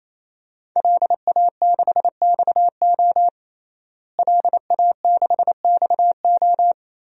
Morse code, heard as a keyed tone, about 28 wpm, 700 Hz. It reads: LA6XO LA6XO
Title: DXCC-100 Call Signs - 5 Characters - Repeated 3x - 28wpm